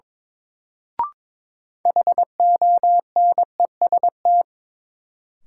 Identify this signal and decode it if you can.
Morse code, heard as a keyed tone, about 21 wpm, 700 Hz. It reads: HONEST
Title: Sets of 5 Words Encore - New Word Drill - 22wpm